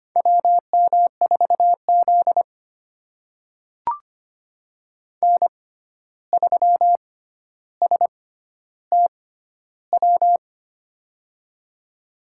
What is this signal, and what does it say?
Morse code, heard as a keyed tone, about 25 wpm, spaced as for 12 wpm, 700 Hz. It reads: WM4Z N3STW
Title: USA Call Signs - 6x Character Spacing - 25wpm